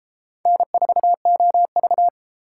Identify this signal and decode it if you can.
Morse code, heard as a keyed tone, about 30 wpm, 700 Hz. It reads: N4OV